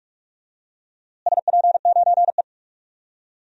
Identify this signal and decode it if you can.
Morse code, heard as a keyed tone, about 46 wpm, 700 Hz. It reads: SP9E
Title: DXCC-100 Call Signs - 4 Characters - Repeated 3x - 45wpm